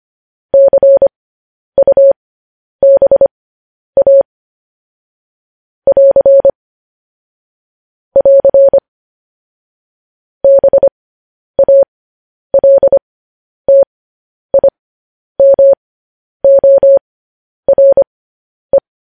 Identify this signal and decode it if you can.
Morse code, heard as a keyed tone, about 25 wpm, spaced as for 10 wpm, 550 Hz. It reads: CUBA <AR> <AR> BALTIMORE